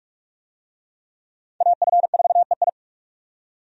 Morse code, heard as a keyed tone, about 41 wpm, 700 Hz. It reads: AF4EI
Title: USA Call Signs - 5 Characters - 45wpm